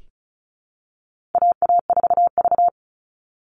Morse code, heard as a keyed tone, about 34 wpm, 700 Hz. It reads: AA4V
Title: Call Signs Repeated 2x - 35wpm